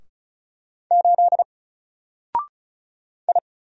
Morse code, heard as a keyed tone, about 34 wpm, 700 Hz. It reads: 8 I
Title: Single Letter-Number Alt 35wpm